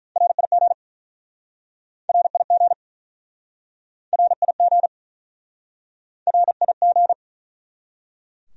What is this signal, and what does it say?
Morse code, heard as a keyed tone, about 49 wpm, 700 Hz. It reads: RIG RIG RIG RIG